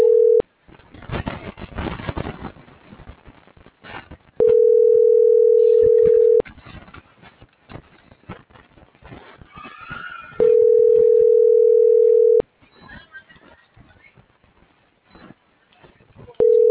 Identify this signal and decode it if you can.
Ringback tone (North America), 440 + 480 Hz.
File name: CENTRAL MABE / out saby.reyes25 - .92400